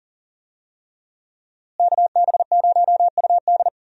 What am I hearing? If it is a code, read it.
Morse code, heard as a keyed tone, about 40 wpm, 700 Hz. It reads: KB0UD
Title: USA Call Signs - 5 Characters - Repeated 3x - 40wpm